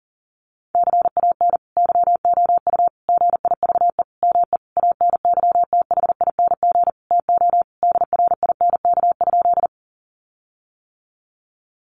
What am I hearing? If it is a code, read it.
Morse code, heard as a keyed tone, about 40 wpm, 700 Hz. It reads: CAN YOU GIVE ME ANYTHING TO DRINK?
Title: Sentences from Top 1500 Words 40wpm